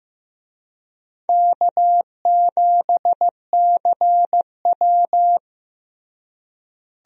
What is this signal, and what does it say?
Morse code, heard as a keyed tone, about 16 wpm, 700 Hz. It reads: K7CW
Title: USA Call Signs - 4 Characters - Repeated 2x - 15wpm